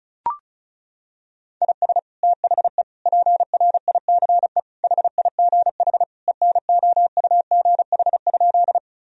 Morse code, heard as a keyed tone, about 34 wpm, 700 Hz. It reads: IS THE PRICE HIGH ENOUGH?
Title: Sentences from Top 1000 Words - Review Part 4 - 35wpm